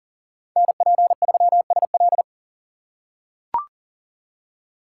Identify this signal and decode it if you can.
Morse code, heard as a keyed tone, about 40 wpm, 700 Hz. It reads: NP3SL
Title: USA Call Signs - 5 Characters - 2x Character Spacing - 40wpm